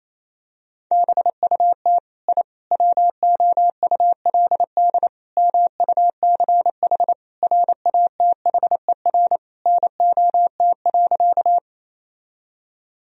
Morse code, heard as a keyed tone, about 28 wpm, 700 Hz. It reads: BUT I WOULD MUCH RATHER NOT.
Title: Sentences from Top 1000 Words - Review Part 4 - 28wpm